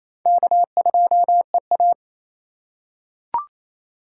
Morse code, heard as a keyed tone, about 27 wpm, 700 Hz. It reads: K2EA
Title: USA Call Signs - 4x Character Spacing - 28wpm